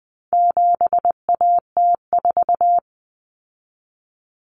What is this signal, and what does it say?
Morse code, heard as a keyed tone, about 20 wpm, 700 Hz. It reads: 7AT4
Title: Common Cut Numbers - 4-Digit 20wpm